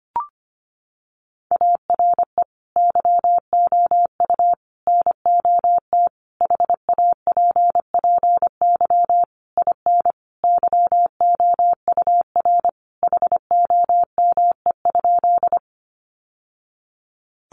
Morse code, heard as a keyed tone, about 25 wpm, 700 Hz. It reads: ARE YOU NOT HAPPY IN YOUR HOME?
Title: Sentences from Top 1000 Words 25wpm